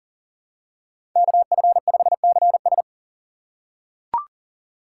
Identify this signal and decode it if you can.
Morse code, heard as a keyed tone, about 40 wpm, 700 Hz. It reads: KF5CS
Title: USA Call Signs - 5 Characters - 2x Character Spacing - 40wpm